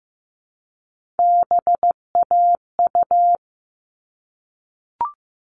Morse code, heard as a keyed tone, about 15 wpm, 700 Hz. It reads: BAU
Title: All Cut Numbers - 3-Digit 15wpm